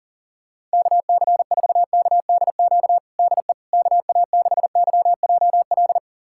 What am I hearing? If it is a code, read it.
Morse code, heard as a keyed tone, about 40 wpm, 700 Hz. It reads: KC4KDQ DE KA6YJL
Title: USA Callsign DE Callsign - 6 Characters - 40wpm